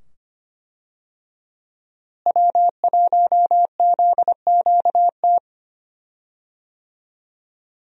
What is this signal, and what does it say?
Morse code, heard as a keyed tone, about 25 wpm, 700 Hz. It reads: W1ZQT
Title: USA Call Signs - 5 Characters - Repeated 2x - 25wpm